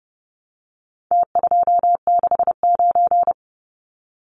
Morse code, heard as a keyed tone, about 30 wpm, 700 Hz. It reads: T269
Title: Numbers - Contest Style - 4-Digit 30wpm